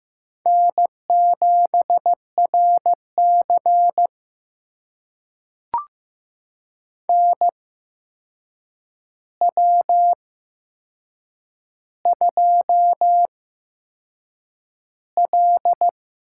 Morse code, heard as a keyed tone, about 15 wpm, spaced as for 6 wpm, 700 Hz. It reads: N7RC NW2L
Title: USA Call Signs - 4 Characters - 8x Character Spacing - 15wpm